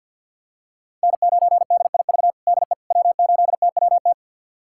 Morse code, heard as a keyed tone, about 53 wpm, 700 Hz. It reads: N9DIV DE W8TWT
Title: USA Callsign DE Callsign - 5 Characters - 50wpm